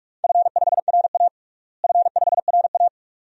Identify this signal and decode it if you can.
Morse code, heard as a keyed tone, about 46 wpm, 700 Hz. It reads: F5RA F5RA
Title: DXCC-100 Call Signs - 4 Characters - Repeated 2x - 45wpm